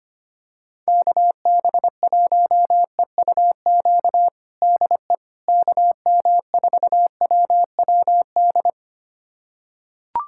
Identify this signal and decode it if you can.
Morse code, heard as a keyed tone, about 25 wpm, 700 Hz. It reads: KB1EUQ DE KM4WWD
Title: USA Callsign DE Callsign - 6 Characters - 25wpm